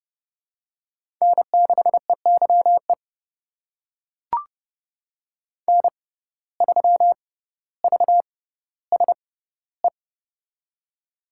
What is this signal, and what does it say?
Morse code, heard as a keyed tone, about 30 wpm, spaced as for 15 wpm, 700 Hz. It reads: N6EYE N3VSE
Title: USA Call Signs - 5 Characters - 6x Character Spacing - 30wpm